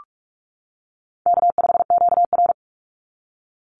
Morse code, heard as a keyed tone, about 44 wpm, 700 Hz. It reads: K5QR
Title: NA SOTA Chaser Callsigns 45wpm